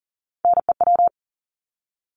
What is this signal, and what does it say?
Morse code, heard as a keyed tone, about 40 wpm, 700 Hz. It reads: NEW
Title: LY Suffix - Top 100 Words with Base Words 40wpm